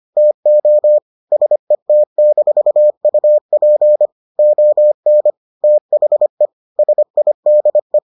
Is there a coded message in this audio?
Yes: Morse code from 0.2 to 8.0 s, heard as a keyed tone, about 25 wpm, 600 Hz. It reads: TO SET-UP ON THE SIDE